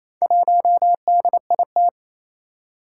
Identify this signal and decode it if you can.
Morse code, heard as a keyed tone, about 28 wpm, 700 Hz. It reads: 1DIT
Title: Four Challenging Letter-Number Combinations 28wpm